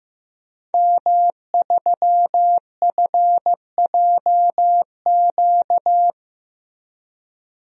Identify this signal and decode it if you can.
Morse code, heard as a keyed tone, about 15 wpm, 700 Hz. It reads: M3FJQ
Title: DXCC-100 Call Signs - 5 Characters - Repeated 2x - 15wpm